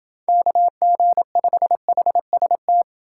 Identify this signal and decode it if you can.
Morse code, heard as a keyed tone, about 29 wpm, 700 Hz. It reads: KG5HST